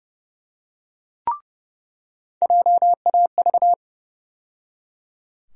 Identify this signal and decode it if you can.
Morse code, heard as a keyed tone, about 30 wpm, 700 Hz. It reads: JAV
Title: USA Call Sign Postfixes - 3 Letters 30wpm